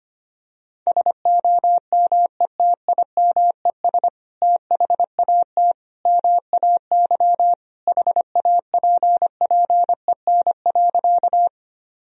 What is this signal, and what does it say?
Morse code, heard as a keyed tone, about 25 wpm, 700 Hz. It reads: SOMETIMES THAT MAY HAPPEN.
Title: Sentences from Top 1000 Words - Review Part 3 - 25wpm